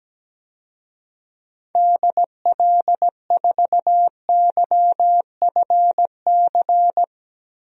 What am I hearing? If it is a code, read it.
Morse code, heard as a keyed tone, about 17 wpm, 700 Hz. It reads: DL4YFC
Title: DXCC-100 Call Signs - 6 Characters - 17wpm